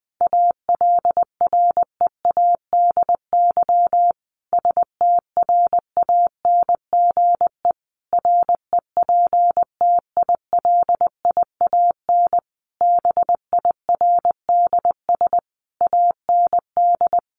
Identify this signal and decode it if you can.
Morse code, heard as a keyed tone, about 20 wpm, 700 Hz. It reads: ALREADY STRANGE REPTILIAN BIRDS AND
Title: A Sound Of Thunder 20wpm